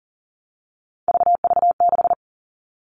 Morse code, heard as a keyed tone, about 40 wpm, 700 Hz. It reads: VV6
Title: All Cut Numbers - 3-Digit 40wpm